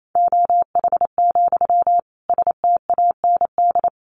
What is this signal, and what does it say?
Morse code, heard as a keyed tone, about 28 wpm, 700 Hz. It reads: OH, STAND
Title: Sentences from Top 1400 Words 28wpm